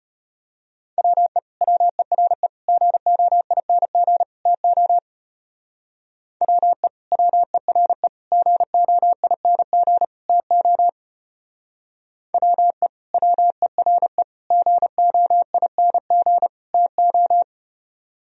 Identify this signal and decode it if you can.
Morse code, heard as a keyed tone, about 33 wpm, 700 Hz. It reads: WE WERE GOING TO WE WERE GOING TO WE WERE GOING TO